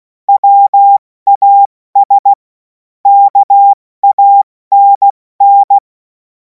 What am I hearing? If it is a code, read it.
Morse code, heard as a keyed tone, about 16 wpm, spaced as for 14 wpm, 800 Hz. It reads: WAS KANN
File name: Morsekurs / Witze / Witz 26. Mai 2026 70 F80